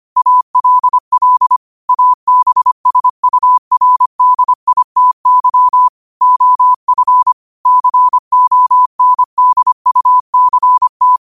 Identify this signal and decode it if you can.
Morse code, heard as a keyed tone, about 25 wpm, 1000 Hz. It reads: ALL ABSURDITY OF CONDUCT